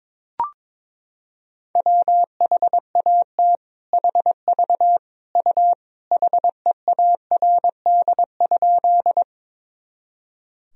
Morse code, heard as a keyed tone, about 22 wpm, 700 Hz. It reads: WHAT HV U HEARD?
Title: Sentences from Top 400 Words and Abbreviations 22wpm